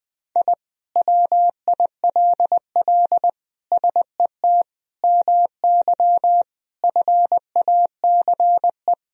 Morse code, heard as a keyed tone, about 20 wpm, 700 Hz. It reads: I WILL SET MY FACE
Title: Sentences from Top 1000 Words - Review Part 2 - 20wpm